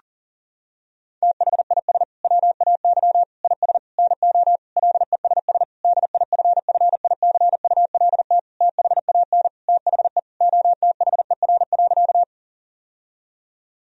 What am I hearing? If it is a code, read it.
Morse code, heard as a keyed tone, about 40 wpm, 700 Hz. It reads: THIS WAY IS NO LESS DIFFICULT THAN THE OTHER.
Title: Sentences from Top 1000 Words - Review Part 2 - 40wpm